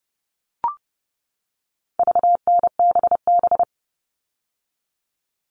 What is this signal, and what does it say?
Morse code, heard as a keyed tone, about 30 wpm, 700 Hz. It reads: VNBB
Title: All Cut Numbers - 4-Digit 30wpm